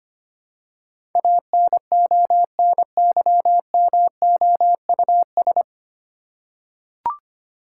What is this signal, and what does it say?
Morse code, heard as a keyed tone, about 25 wpm, 700 Hz. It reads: ANONYMOUS
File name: practice / 25wpm / 9 Letter Words - First 3 Letters Given - 25wpm